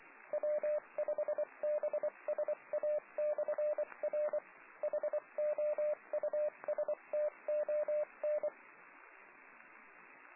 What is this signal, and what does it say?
Morse code, heard as a keyed tone, about 24 wpm, 600 Hz, with radio noise. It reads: W5BSA/R HOUSTON